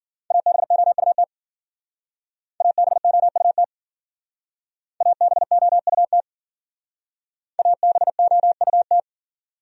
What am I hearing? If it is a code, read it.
Morse code, heard as a keyed tone, about 52 wpm, 700 Hz. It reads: ABOUT ABOUT ABOUT ABOUT